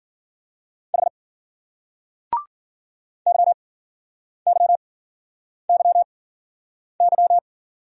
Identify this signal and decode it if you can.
Morse code, heard as a keyed tone, about 53 wpm, spaced as for 9 wpm, 700 Hz. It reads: H YYYY